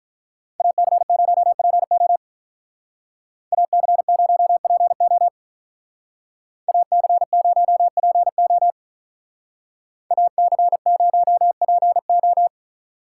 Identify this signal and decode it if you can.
Morse code, heard as a keyed tone, about 43 wpm, 700 Hz. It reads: AC0PO AC0PO AC0PO AC0PO
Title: Speed Racing - USA 2x2 Call Signs 35wpm